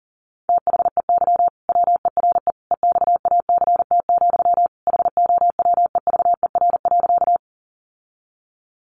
Morse code, heard as a keyed tone, about 40 wpm, 700 Hz. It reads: THEY WERE EXACT, HOWEVER.
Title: Sentences from Top 1300 Words 40wpm